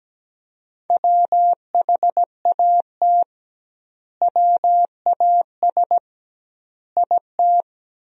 Morse code, heard as a keyed tone, about 17 wpm, 700 Hz. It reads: WHAT WAS IT
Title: 2x Word-Spacing - Sentences from Top 200 Words - 17wpm